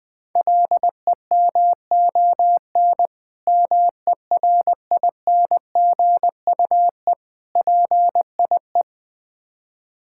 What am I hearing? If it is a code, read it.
Morse code, heard as a keyed tone, about 20 wpm, 700 Hz. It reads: LEMON MERINGUE PIE
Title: Difficult Words with Hints - 20wpm